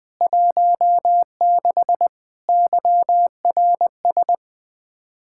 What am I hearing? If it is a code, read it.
Morse code, heard as a keyed tone, about 20 wpm, 700 Hz. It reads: 16 YRS
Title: Practice QSO - Exchange 3 - More Info - 20wpm